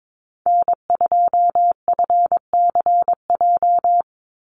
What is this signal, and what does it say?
Morse code, heard as a keyed tone, about 22 wpm, 700 Hz. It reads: N2FCJ
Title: USA Call Signs 22wpm